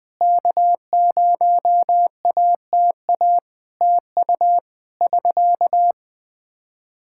Morse code, heard as a keyed tone, about 20 wpm, 700 Hz. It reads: K0ATA TU <SK>
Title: Practice QSO - Exchange 4 - Ending - 20wpm